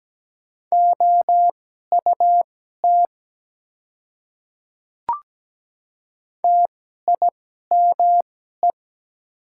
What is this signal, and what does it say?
Morse code, heard as a keyed tone, about 17 wpm, spaced as for 12 wpm, 700 Hz. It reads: OUT TIME
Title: 2x Character Spacing - Top 100 Words - 17wpm